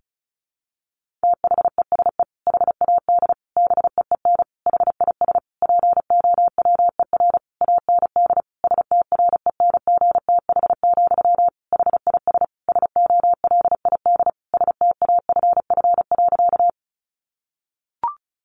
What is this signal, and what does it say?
Morse code, heard as a keyed tone, about 34 wpm, 700 Hz. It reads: THESE HAD BEEN HIS POWER AND STRENGTH, HIS SOLID STAFF.
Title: Sentences from Top 1500 Words 35wpm